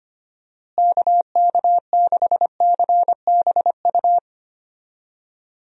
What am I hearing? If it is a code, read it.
Morse code, heard as a keyed tone, about 25 wpm, 700 Hz. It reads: KK6CBU
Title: USA Call Signs - 6 Characters - Repeated 3x - 25wpm